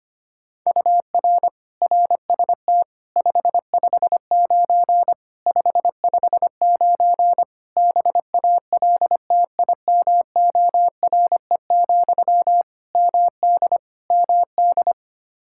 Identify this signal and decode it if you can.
Morse code, heard as a keyed tone, about 25 wpm, 700 Hz. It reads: UR RST 559 559 BALTIMORE, MD MD
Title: Practice QSO - Exchange 1 - Beginning - 25wpm